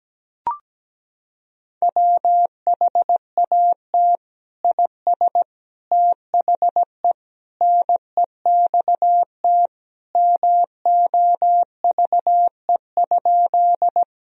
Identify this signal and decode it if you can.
Morse code, heard as a keyed tone, about 17 wpm, 700 Hz. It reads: WHAT IS THE NEXT MOVE?
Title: Sentences from Top 1000 Words - Review Part 1 - 17wpm